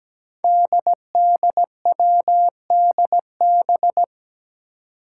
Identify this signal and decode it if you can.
Morse code, heard as a keyed tone, about 17 wpm, 700 Hz. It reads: DDWDB
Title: Five Challenging Letter-Number Combinations 17wpm